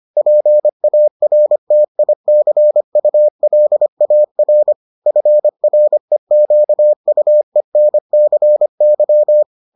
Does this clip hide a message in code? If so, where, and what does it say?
Yes: Morse code from 0.2 to 9.4 s, heard as a keyed tone, about 25 wpm, 600 Hz. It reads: PARTICULAR FREQUENCY